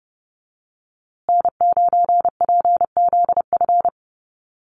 Morse code, heard as a keyed tone, about 30 wpm, 700 Hz. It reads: N9PZF
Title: USA Call Signs Repeated 3x 30wpm